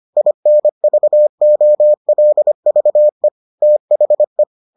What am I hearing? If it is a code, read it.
Morse code, heard as a keyed tone, about 25 wpm, 600 Hz. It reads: INVOLVE THE